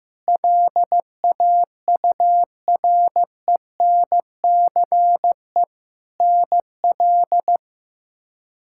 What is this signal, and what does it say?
Morse code, heard as a keyed tone, about 15 wpm, 700 Hz. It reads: LAURENCE NL
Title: K1USN SST QSO Practice - 15wpm